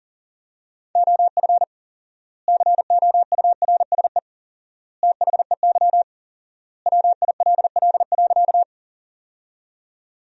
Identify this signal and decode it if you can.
Morse code, heard as a keyed tone, about 40 wpm, 700 Hz. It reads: OF COURSE THEY WILL.
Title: 4x Word-Spacing - Sentences from Top 500 Words - 40wpm